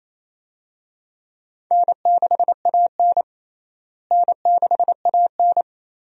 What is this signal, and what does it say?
Morse code, heard as a keyed tone, about 27 wpm, 700 Hz. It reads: N6AN N6AN
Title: USA Call Signs - 4 Characters - Repeated 2x - 28wpm